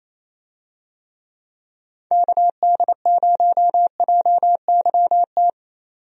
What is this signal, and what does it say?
Morse code, heard as a keyed tone, about 28 wpm, 700 Hz. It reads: KD0JYT
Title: USA Call Signs - 6 Characters - 28wpm